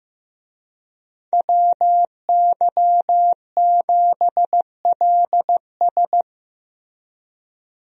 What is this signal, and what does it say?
Morse code, heard as a keyed tone, about 15 wpm, 700 Hz. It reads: WY7LS
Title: USA Call Signs - 5 Characters - Repeated 2x - 15wpm